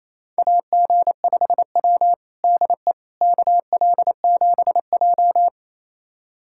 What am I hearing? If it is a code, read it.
Morse code, heard as a keyed tone, about 27 wpm, 690 Hz. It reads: AG5W DE KL7J
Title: USA Callsign DE Callsign - 4 Characters - 28wpm